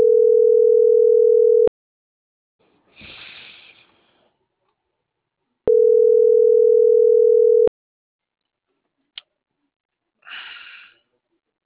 Ringback tone (North America), 440 + 480 Hz.